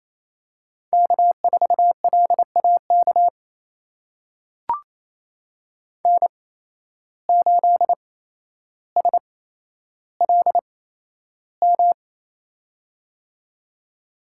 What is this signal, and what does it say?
Morse code, heard as a keyed tone, about 28 wpm, spaced as for 12 wpm, 700 Hz. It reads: K4LAK N8SLM
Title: USA Call Signs - 5 Characters - 8x Character Spacing - 28wpm